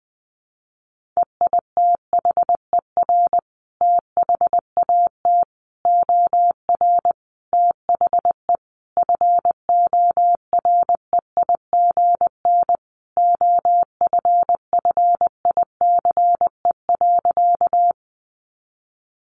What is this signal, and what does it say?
Morse code, heard as a keyed tone, about 20 wpm, 700 Hz. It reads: EITHER THAT OR THE FOREIGN OFFICE.
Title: Sentences from Top 1400 Words 20wpm